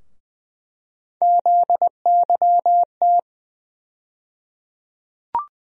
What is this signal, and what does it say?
Morse code, heard as a keyed tone, about 20 wpm, 700 Hz. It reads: ZYT
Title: USA Call Sign Postfixes - 3 Letters 20wpm